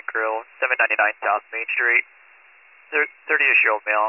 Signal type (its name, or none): Emergency Alert System (SAME) tones